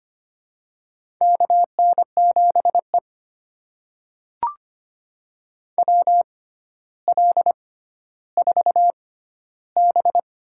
Morse code, heard as a keyed tone, about 25 wpm, spaced as for 13 wpm, 700 Hz. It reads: KN7E WL4B
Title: USA Call Signs - 4 Characters - 6x Character Spacing - 25wpm